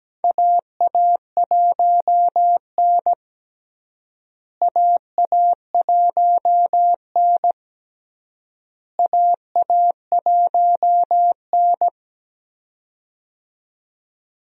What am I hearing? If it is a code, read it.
Morse code, heard as a keyed tone, about 17 wpm, 700 Hz. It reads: AA1N AA1N AA1N
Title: USA Call Signs - 4 Characters - Repeated 3x - 17wpm